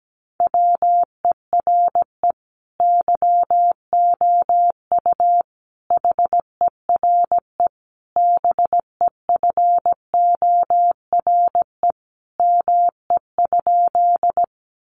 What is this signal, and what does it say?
Morse code, heard as a keyed tone, about 17 wpm, 700 Hz. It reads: WERE YOU HERE BEFORE ME?
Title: Sentences from Top 200 Words 17wpm